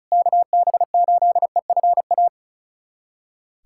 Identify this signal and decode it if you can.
Morse code, heard as a keyed tone, about 34 wpm, 700 Hz. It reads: KB8EFA
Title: USA Callsign DE Callsign - 6 Characters - 35wpm